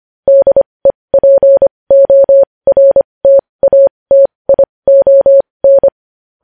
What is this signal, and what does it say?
Morse code, heard as a keyed tone, about 25 wpm, spaced as for 20 wpm, 550 Hz. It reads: DEPORTATION
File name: podcast / News headlines-1x-25x20